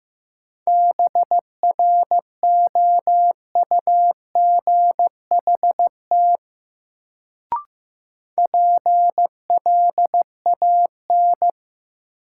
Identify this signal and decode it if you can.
Morse code, heard as a keyed tone, about 15 wpm, 700 Hz. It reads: BROUGHT PLAN
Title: Sentences from Top 500 Words - New Word Drill - 15wpm